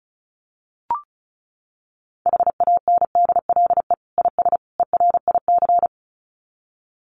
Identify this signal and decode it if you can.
Morse code, heard as a keyed tone, about 34 wpm, 700 Hz. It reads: HANDLE IS ERIC
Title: Common QSO Phrases 35wpm